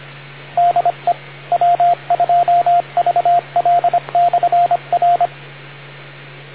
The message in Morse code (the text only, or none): DE W2VL/R